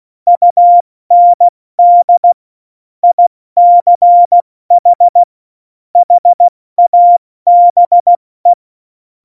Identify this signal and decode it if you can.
Morse code, heard as a keyed tone, about 16 wpm, 700 Hz. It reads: UND ICH HABE
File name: Morsekurs / Witze / Witz 19 Mai, 2020 70 F80